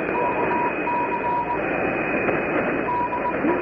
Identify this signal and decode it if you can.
Morse code, heard as a keyed tone, about 20 wpm, 940 Hz, with radio noise. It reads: KC DE
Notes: R. Bambamarca/Amistad 8515 0336, 0400 UTC Shortwave Radio Audio